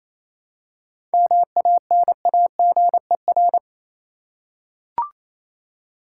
Morse code, heard as a keyed tone, about 28 wpm, 700 Hz. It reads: MANAGER
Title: Seven Letter Words - 28wpm